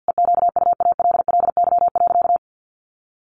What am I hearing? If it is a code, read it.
Morse code, heard as a keyed tone, about 50 wpm, 700 Hz. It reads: EQUALLY.
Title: Sentences from Top 1400 Words 50wpm